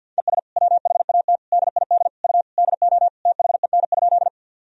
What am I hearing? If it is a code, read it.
Morse code, heard as a keyed tone, about 52 wpm, 700 Hz. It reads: ES WHAT DID U DO THEN?
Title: Sentences from Top 100 Words and Abbreviations 50wpm